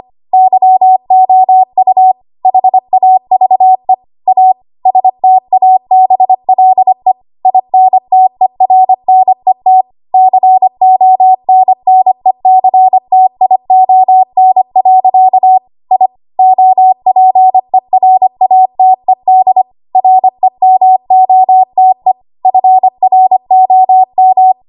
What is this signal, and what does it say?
Morse code, heard as a keyed tone, about 25 wpm, 750 Hz. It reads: YOU HAVE A STABLE INTERNET CONNECTION. I OPERATED REMOTE FROM
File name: Archive / 25 WPM / 25WPM